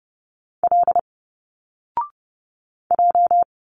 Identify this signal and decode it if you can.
Morse code, heard as a keyed tone, about 30 wpm, 700 Hz. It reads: L J
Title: Single Letters 30wpm